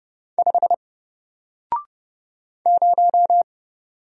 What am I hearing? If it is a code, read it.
Morse code, heard as a keyed tone, about 30 wpm, 700 Hz. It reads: 5 0
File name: Single-Letter-Number-30wpm